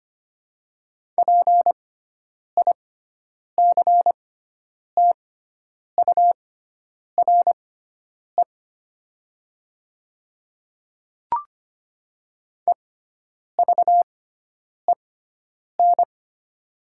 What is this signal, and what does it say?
Morse code, heard as a keyed tone, about 25 wpm, spaced as for 9 wpm, 700 Hz. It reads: PICTURE EVEN
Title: 6x Character Spacing - Top 500 Words - 25wpm